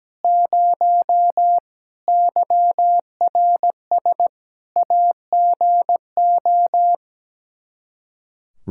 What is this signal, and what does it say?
Morse code, heard as a keyed tone, about 17 wpm, 700 Hz. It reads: 0 YRS AGO
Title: Practice QSO - Exchange 3 - More Info - 17wpm